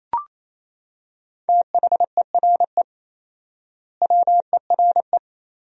Morse code, heard as a keyed tone, about 28 wpm, 700 Hz. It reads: THERE WERE
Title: 4x Word-Spacing - Sentences from Top 100 Words - 28wpm